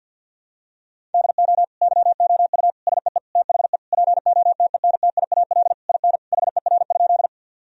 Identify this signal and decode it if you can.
Morse code, heard as a keyed tone, about 50 wpm, 700 Hz. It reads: DO YOU SEE THE POTENTIAL IN HER?
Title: Sentences from Top 1000 Words - Review Part 4 - 50wpm